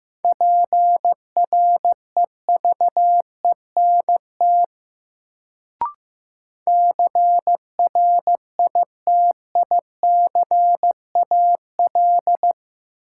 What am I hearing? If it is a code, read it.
Morse code, heard as a keyed tone, about 15 wpm, 700 Hz. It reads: PREVENT CRITICAL
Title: Sentences from Top 1100 Words - New Word Drill - 15wpm